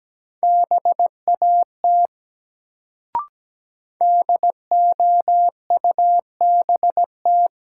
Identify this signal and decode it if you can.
Morse code, heard as a keyed tone, about 17 wpm, 700 Hz. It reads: BAT DOUBT
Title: Sentences from Top 1500 Words - New Word Drill - 17wpm